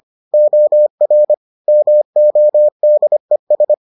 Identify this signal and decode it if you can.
Morse code, heard as a keyed tone, about 25 wpm, 600 Hz. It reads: OR MODES